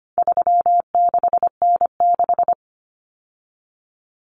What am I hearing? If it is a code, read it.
Morse code, heard as a keyed tone, about 25 wpm, 700 Hz. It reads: 36N6
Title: Common Cut Numbers - 4-Digit 25wpm